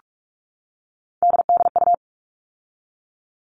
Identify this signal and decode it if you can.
Morse code, heard as a keyed tone, about 44 wpm, 700 Hz. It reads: DDU
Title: All Cut Numbers - 3-Digit 45wpm